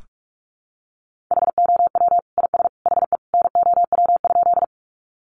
Morse code, heard as a keyed tone, about 44 wpm, 700 Hz. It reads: HOW IS HE NOW?
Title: Sentences from Top 100 Words 45wpm